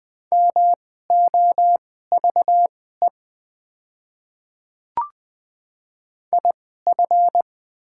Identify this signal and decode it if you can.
Morse code, heard as a keyed tone, about 20 wpm, spaced as for 14 wpm, 700 Hz. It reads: MOVE IF
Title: 2x Character Spacing - Top 300 Words - 20wpm